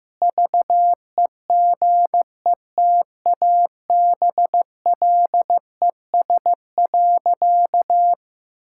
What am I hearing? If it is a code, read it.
Morse code, heard as a keyed tone, about 15 wpm, 700 Hz. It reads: VEGETABLES.
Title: Pangram Challenge v2 - 15wpm